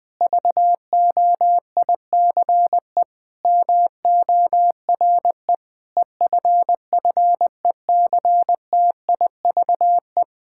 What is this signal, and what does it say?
Morse code, heard as a keyed tone, about 20 wpm, 700 Hz. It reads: VOICE MORE EFFECTIVE
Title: Sentences from Top 1000 Words - Review Part 6 - 20wpm